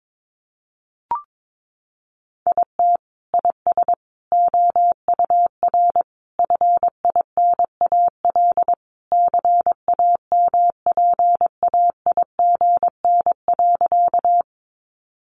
Morse code, heard as a keyed tone, about 22 wpm, 700 Hz. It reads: IT IS OUR FINAL CAMPAIGN.
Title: Sentences from Top 1200 Words 22wpm